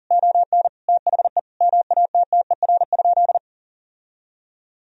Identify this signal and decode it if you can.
Morse code, heard as a keyed tone, about 40 wpm, 700 Hz. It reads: ON THE MATTER?
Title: Sentences from Top 1000 Words - Review Part 4 - 40wpm